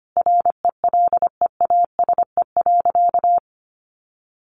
Morse code, heard as a keyed tone, about 25 wpm, 700 Hz. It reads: RELEASE.
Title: Sentences from Top 1100 Words 25wpm